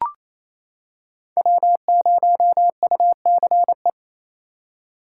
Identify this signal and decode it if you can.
Morse code, heard as a keyed tone, about 28 wpm, 700 Hz. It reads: W0UCE
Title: K1USN SST QSO Practice - 28wpm